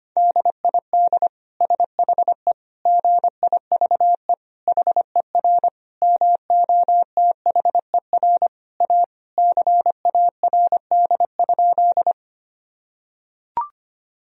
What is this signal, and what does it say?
Morse code, heard as a keyed tone, about 25 wpm, 700 Hz. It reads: DID SHE GIVE HER MOTHER A CARD?
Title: Sentences from Top 1000 Words - Review Part 4 - 25wpm